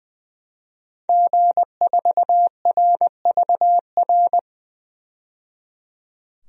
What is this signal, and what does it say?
Morse code, heard as a keyed tone, about 20 wpm, 700 Hz. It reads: G4RVR
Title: DXCC-100 Call Signs - 5 Characters - 20wpm